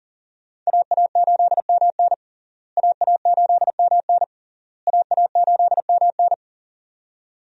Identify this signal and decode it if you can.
Morse code, heard as a keyed tone, about 40 wpm, 700 Hz. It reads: AA8MN AA8MN AA8MN
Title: USA Call Signs - 5 Characters - Repeated 3x - 40wpm